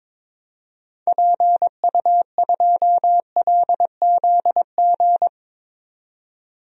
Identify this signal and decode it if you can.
Morse code, heard as a keyed tone, about 21 wpm, 700 Hz. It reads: PU2LZG
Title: DXCC-100 Call Signs - 6 Characters - 22wpm